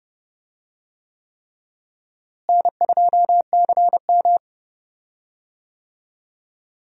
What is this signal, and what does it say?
Morse code, heard as a keyed tone, about 30 wpm, 700 Hz. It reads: N2CM
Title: USA Call Signs - 4 Characters - Repeated 3x - 30wpm